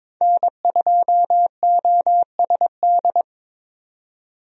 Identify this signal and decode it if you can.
Morse code, heard as a keyed tone, about 22 wpm, 700 Hz. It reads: N2OSD
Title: USA Call Signs - 5 Characters - 22wpm